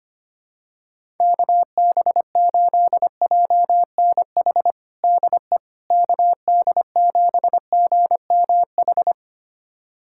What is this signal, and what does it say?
Morse code, heard as a keyed tone, about 25 wpm, 700 Hz. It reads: KB8JNH DE KD7GMH
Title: USA Callsign DE Callsign - 6 Characters - 25wpm